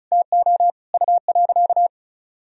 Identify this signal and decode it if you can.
Morse code, heard as a keyed tone, about 36 wpm, 700 Hz. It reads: TO U.
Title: Sentences from Top 500 Words and Abbreviations 35wpm